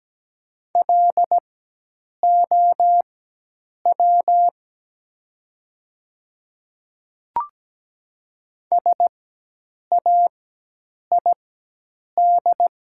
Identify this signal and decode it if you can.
Morse code, heard as a keyed tone, about 17 wpm, spaced as for 8 wpm, 700 Hz. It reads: LOW SAID
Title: 4x Character Spacing - Top 200 Words - 17wpm